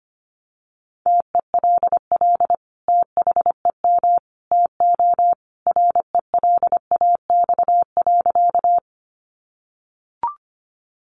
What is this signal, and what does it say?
Morse code, heard as a keyed tone, about 25 wpm, 700 Hz. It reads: TELL THEM TO RELAX.
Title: Sentences from Top 1400 Words 25wpm